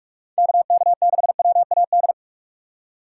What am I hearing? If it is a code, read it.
Morse code, heard as a keyed tone, about 42 wpm, 700 Hz. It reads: KK6WAD
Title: USA Call Signs - 6 Characters - 2x Character Spacing - 45wpm